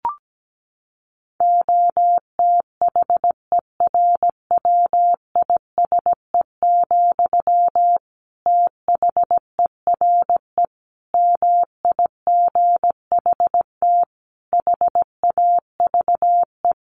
Morse code, heard as a keyed tone, about 17 wpm, 700 Hz. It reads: OTHERWISE, THERE MIGHT HAVE
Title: Sentences from Top 1400 Words 17wpm